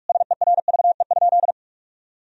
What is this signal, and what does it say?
Morse code, heard as a keyed tone, about 46 wpm, 700 Hz. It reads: SERVE?
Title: Sentences from Top 1000 Words - Review Part 2 - 45wpm